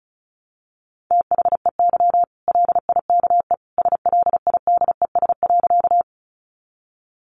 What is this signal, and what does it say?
Morse code, heard as a keyed tone, about 34 wpm, 700 Hz. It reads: THEY LIKE SLIDES.
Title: Easy Phrases 35wpm